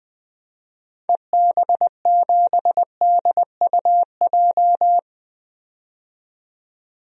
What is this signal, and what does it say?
Morse code, heard as a keyed tone, about 21 wpm, 700 Hz. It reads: EB7DUJ
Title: DXCC-100 Call Signs - 6 Characters - 20wpm